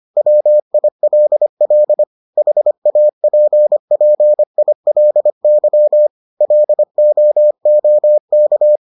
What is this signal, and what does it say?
Morse code, heard as a keyed tone, about 25 wpm, 600 Hz. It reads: WILL HAPPILY LOOK